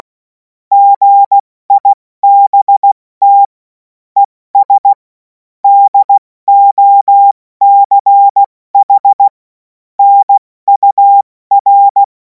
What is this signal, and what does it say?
Morse code, heard as a keyed tone, about 16 wpm, spaced as for 14 wpm, 800 Hz. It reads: GIBT ES DOCH NUR
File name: Morsekurs / Witze / Witz 03. März 2026 70 F80